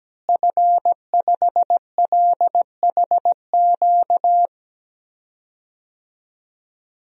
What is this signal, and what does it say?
Morse code, heard as a keyed tone, about 17 wpm, 700 Hz. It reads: F5LHQ
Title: DXCC-100 Call Signs - 5 Characters - Repeated 3x - 17wpm